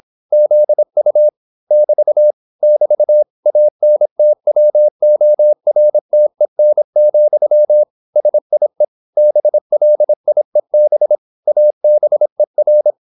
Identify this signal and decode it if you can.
Morse code, heard as a keyed tone, about 26 wpm, 600 Hz. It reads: ZU <BT> <BT> ANTWORTEN, SIE BLIEB ABER